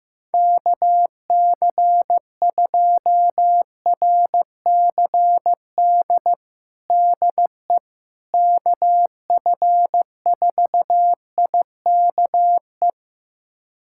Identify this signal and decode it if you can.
Morse code, heard as a keyed tone, about 15 wpm, 700 Hz. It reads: KC2RCD DE KF4IKE
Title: USA Callsign DE Callsign 15wpm